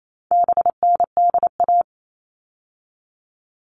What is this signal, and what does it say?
Morse code, heard as a keyed tone, about 28 wpm, 700 Hz. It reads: BNDA
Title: All Cut Numbers - 4-Digit 28wpm